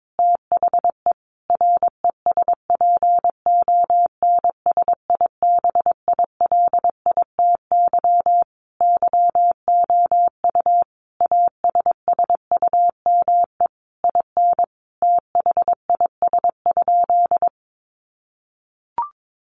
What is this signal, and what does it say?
Morse code, heard as a keyed tone, about 22 wpm, 700 Hz. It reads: THE RESPONSIBILITY YOU ASSUME IN THIS?
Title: Sentences from Top 1500 Words 22wpm